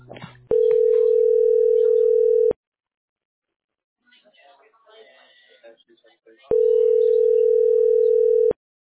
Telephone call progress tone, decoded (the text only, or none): ringback tone